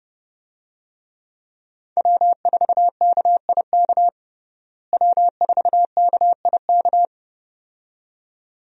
Morse code, heard as a keyed tone, about 32 wpm, 700 Hz. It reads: W4KIK W4KIK
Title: USA Call Signs - 5 Characters - Repeated 2x - 30wpm